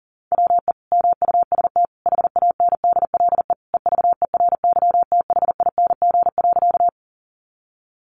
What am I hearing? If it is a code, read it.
Morse code, heard as a keyed tone, about 40 wpm, 700 Hz. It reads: WE MUST HANDLE EVERYTHING.
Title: Sentences from Top 1100 Words 40wpm